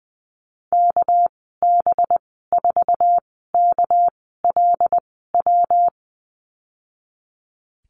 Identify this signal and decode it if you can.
Morse code, heard as a keyed tone, about 20 wpm, spaced as for 14 wpm, 700 Hz. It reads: KB4KLW
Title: 213 Lesson 34g - Number 8 - Call Signs 20wpm